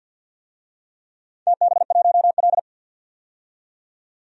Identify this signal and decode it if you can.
Morse code, heard as a keyed tone, about 51 wpm, 700 Hz. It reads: TB1L
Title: DXCC-100 Call Signs - 4 Characters - 50wpm